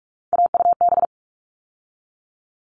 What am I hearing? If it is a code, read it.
Morse code, heard as a keyed tone, about 44 wpm, 700 Hz. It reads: AUB
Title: All Cut Numbers - 3-Digit 45wpm